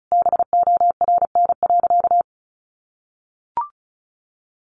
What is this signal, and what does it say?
Morse code, heard as a keyed tone, about 34 wpm, 700 Hz. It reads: BORN.
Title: Sentences from Top 1300 Words 35wpm